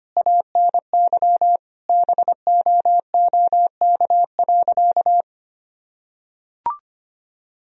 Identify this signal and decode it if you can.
Morse code, heard as a keyed tone, about 25 wpm, 700 Hz. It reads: ANY BOOK.
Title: Sentences from Top 1000 Words - Review Part 1 - 25wpm